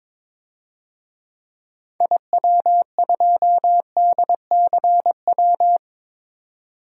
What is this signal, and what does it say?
Morse code, heard as a keyed tone, about 21 wpm, 700 Hz. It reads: IW2DCW
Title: DXCC-100 Call Signs - 6 Characters - 22wpm